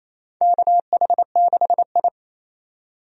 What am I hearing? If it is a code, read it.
Morse code, heard as a keyed tone, about 27 wpm, 700 Hz. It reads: KH6I
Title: DXCC-100 Call Signs - 4 Characters - 28wpm